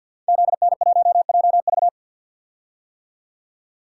Morse code, heard as a keyed tone, about 52 wpm, 700 Hz. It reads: ZN1JV
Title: Code Groups - 5 Letter:Numbers - 50wpm